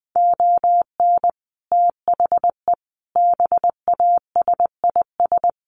Morse code, heard as a keyed tone, about 20 wpm, 700 Hz. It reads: ON THE BASIS
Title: Sets of 5 Words 20wpm